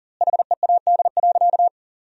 Morse code, heard as a keyed tone, about 40 wpm, 700 Hz. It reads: HEAD.
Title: 2x Word-Spacing - Sentences from Top 300 Words - 40wpm